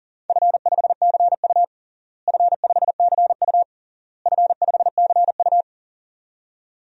Morse code, heard as a keyed tone, about 40 wpm, 700 Hz. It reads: F5CU F5CU F5CU
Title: DXCC-100 Call Signs - 4 Characters - Repeated 3x - 40wpm